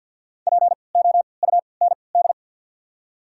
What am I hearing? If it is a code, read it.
Morse code, heard as a keyed tone, about 48 wpm, spaced as for 28 wpm, 700 Hz. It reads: POUND
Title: 3x Character Spacing - Top 500 Words - 50wpm